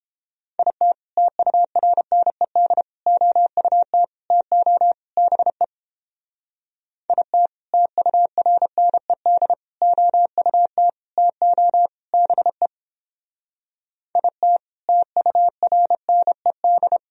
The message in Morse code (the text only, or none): IT TURNED OUT TO BE IT TURNED OUT TO BE IT TURNED